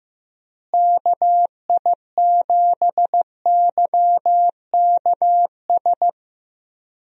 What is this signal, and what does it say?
Morse code, heard as a keyed tone, about 15 wpm, 700 Hz. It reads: KI7YKS
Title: USA Call Signs - 6 Characters - Repeated 3x - 15wpm